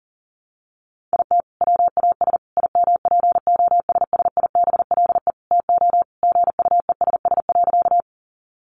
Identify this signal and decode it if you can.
Morse code, heard as a keyed tone, about 40 wpm, 700 Hz. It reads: IT WAS IMPOSSIBLE TO GUESS.
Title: Sentences from Top 1500 Words 40wpm